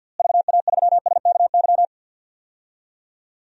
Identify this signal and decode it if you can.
Morse code, heard as a keyed tone, about 49 wpm, 700 Hz. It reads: VA3SKY
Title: DXCC-100 Call Signs - 6 Characters - Repeated 3x - 50wpm